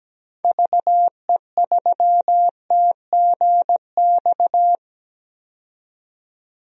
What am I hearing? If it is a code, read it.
Morse code, heard as a keyed tone, about 17 wpm, 700 Hz. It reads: VE3TGX
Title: DXCC-100 Call Signs - 6 Characters - 17wpm